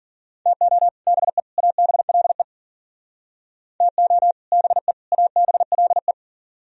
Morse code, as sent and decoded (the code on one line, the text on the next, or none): - --- / -... / .- -... .-.. / - --- / -... / .- -... .-.. .
TO BE ABLE TO BE ABLE